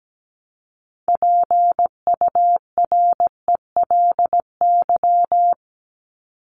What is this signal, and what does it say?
Morse code, heard as a keyed tone, about 17 wpm, 700 Hz. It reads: PURELY
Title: LY Suffix - Top 100 Words with Base Words 17wpm